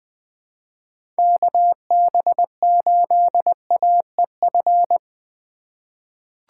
Morse code, heard as a keyed tone, about 20 wpm, 700 Hz. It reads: KB8AEF
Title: USA Call Signs - 6 Characters - 3x Character Spacing - 20wpm